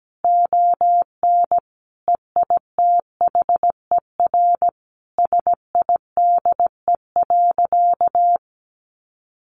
Morse code, heard as a keyed tone, about 17 wpm, 700 Hz. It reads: ON EITHER SIDE.
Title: Sentences from Top 1200 Words 17wpm